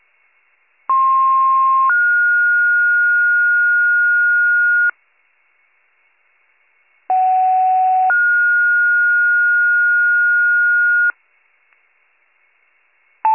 Two-tone page, heard as a one-second tone, then a three-second tone, about 1060 then 1430 Hz, about 750 then 1430 Hz. It is heard from 0.9 to 11.1 s.